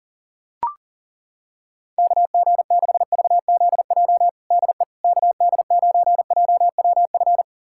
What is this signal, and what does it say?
Morse code, heard as a keyed tone, about 37 wpm, 700 Hz. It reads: KG6VZJ DE KD9JWF
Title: USA Callsign DE Callsign - 6 Characters - 40wpm